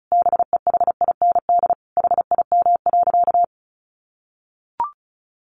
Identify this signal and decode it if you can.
Morse code, heard as a keyed tone, about 34 wpm, 700 Hz. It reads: BEHIND HIM.
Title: Sentences from Top 700 Words 35wpm